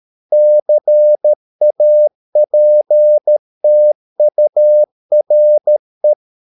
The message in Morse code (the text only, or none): CAPTURE